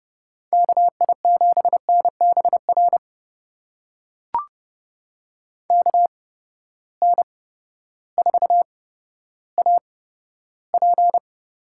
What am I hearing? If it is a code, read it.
Morse code, heard as a keyed tone, about 30 wpm, spaced as for 14 wpm, 700 Hz. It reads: KI7NBR KN4AP
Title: USA Call Signs - 6 Characters - 8x Character Spacing - 30wpm